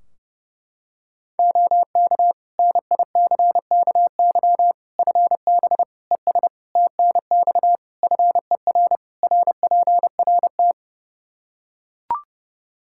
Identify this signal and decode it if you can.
Morse code, heard as a keyed tone, about 30 wpm, 700 Hz. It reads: OK NICKY FB ES TNX FER RPRT
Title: Practice QSO - Exchange 2 - Info - 30wpm